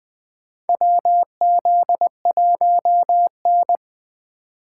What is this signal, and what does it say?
Morse code, heard as a keyed tone, about 20 wpm, 700 Hz. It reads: WZ1N
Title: USA Call Signs - 4 Characters - 2x Character Spacing - 20wpm